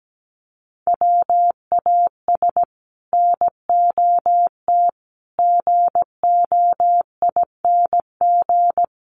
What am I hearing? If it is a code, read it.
Morse code, heard as a keyed tone, about 17 wpm, 700 Hz. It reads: WAS NOT GOING
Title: Sets of 5 Words Encore 17wpm